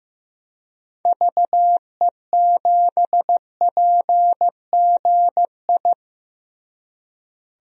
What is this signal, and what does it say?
Morse code, heard as a keyed tone, about 15 wpm, 700 Hz. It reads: VE7PGI
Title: DXCC-100 Call Signs - 6 Characters - Repeated 2x - 15wpm